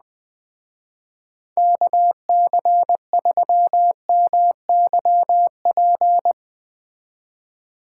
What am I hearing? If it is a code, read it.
Morse code, heard as a keyed tone, about 20 wpm, 700 Hz. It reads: KC3MYP
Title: Speed Racing - USA 2x3 Call Signs 17wpm